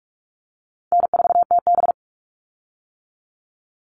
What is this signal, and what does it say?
Morse code, heard as a keyed tone, about 44 wpm, 700 Hz. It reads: N4TB
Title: Easy Call Signs - 45wpm